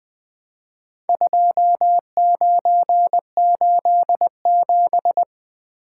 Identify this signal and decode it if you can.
Morse code, heard as a keyed tone, about 20 wpm, 700 Hz. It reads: 2987
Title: Four Letter-Number Combinations - 20wpm